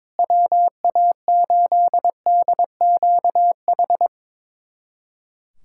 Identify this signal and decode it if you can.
Morse code, heard as a keyed tone, about 22 wpm, 700 Hz. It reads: WA8DQH
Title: USA Callsign DE Callsign 22wpm